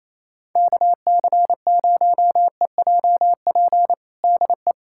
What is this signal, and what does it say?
Morse code, heard as a keyed tone, about 28 wpm, 700 Hz. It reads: KC0EJP DE
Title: USA Callsign DE Callsign - 6 Characters - 28wpm